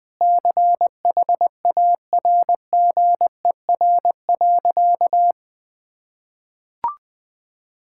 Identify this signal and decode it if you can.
Morse code, heard as a keyed tone, about 20 wpm, 700 Hz. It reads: CHARGER.
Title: Sentences with 6-Word Phrases - 20wpm